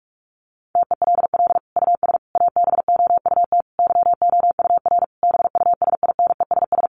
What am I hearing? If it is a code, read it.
Morse code, heard as a keyed tone, about 44 wpm, 700 Hz. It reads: TELL US ABOUT YOUR BUSINESS
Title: Sentences from Top 1200 Words 45wpm